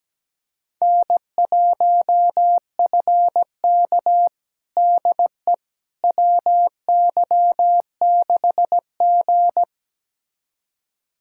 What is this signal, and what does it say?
Morse code, heard as a keyed tone, about 17 wpm, 700 Hz. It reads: N1FK DE WY6G
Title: USA Callsign DE Callsign - 4 Characters - 17wpm